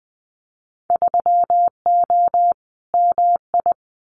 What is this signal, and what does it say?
Morse code, heard as a keyed tone, about 20 wpm, 700 Hz. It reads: 3O MI
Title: Winter Field Day Exchanges 20wpm